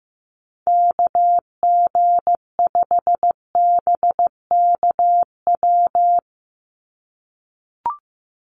Morse code, heard as a keyed tone, about 15 wpm, 700 Hz. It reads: KG5BKW
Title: USA Call Signs 15wpm